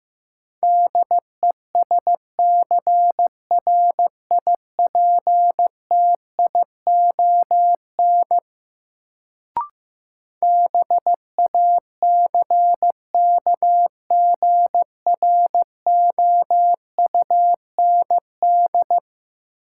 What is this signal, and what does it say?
Morse code, heard as a keyed tone, about 15 wpm, 700 Hz. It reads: DESCRIPTION BACKGROUND
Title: Sentences from Top 1300 Words - New Word Drill - 15wpm